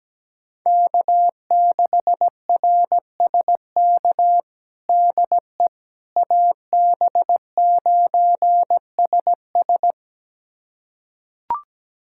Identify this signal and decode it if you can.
Morse code, heard as a keyed tone, about 17 wpm, 700 Hz. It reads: K6RSK DE AB9SS
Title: USA Callsign DE Callsign - 5 Characters - 17wpm